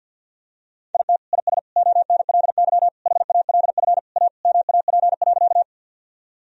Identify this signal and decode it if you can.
Morse code, heard as a keyed tone, about 49 wpm, 700 Hz. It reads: IT IS ONLY HALF A MAP.
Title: Sentences from Top 1000 Words - Review Part 2 - 50wpm